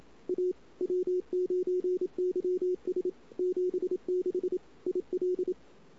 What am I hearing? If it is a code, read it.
Morse code, heard as a keyed tone, about 32 wpm, 410 Hz. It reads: A W9YS 76 IL
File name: CW / 2008-ss-cw-k5tr-op-n5rz -22:00